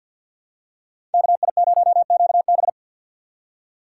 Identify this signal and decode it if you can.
Morse code, heard as a keyed tone, about 49 wpm, 700 Hz. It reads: KI0QB
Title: USA Call Signs - 5 Characters - 6x Character Spacing - 50wpm